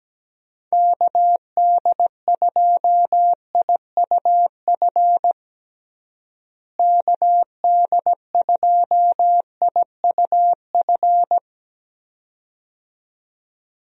Morse code, heard as a keyed tone, about 17 wpm, 700 Hz. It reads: KD2IUF KD2IUF
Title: USA Call Signs - 6 Characters - Repeated 2x - 17wpm